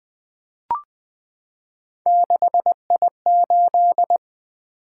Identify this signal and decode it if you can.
Morse code, heard as a keyed tone, about 20 wpm, 700 Hz. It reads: 6I8
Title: Three Letter-Number Combinations - 20wpm